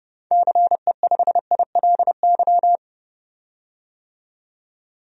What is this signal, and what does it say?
Morse code, heard as a keyed tone, about 29 wpm, 700 Hz. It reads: CE5ILY
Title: DXCC-100 Call Signs - 6 Characters - 30wpm